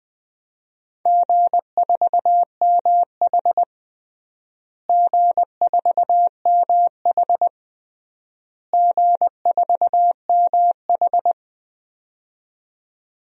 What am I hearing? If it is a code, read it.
Morse code, heard as a keyed tone, about 20 wpm, 700 Hz. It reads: G4MH G4MH G4MH
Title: DXCC-100 Call Signs - 4 Characters - Repeated 3x - 20wpm